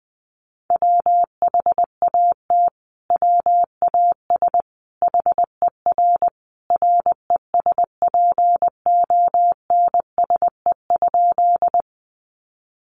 Morse code, heard as a keyed tone, about 20 wpm, 700 Hz. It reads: WHAT WAS HER RESPONSE?
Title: Sentences from Top 1400 Words 20wpm